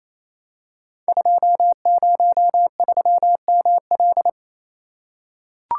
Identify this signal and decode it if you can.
Morse code, heard as a keyed tone, about 28 wpm, 700 Hz. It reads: 203ML
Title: Code Groups - 5 Letter:Numbers - 28wpm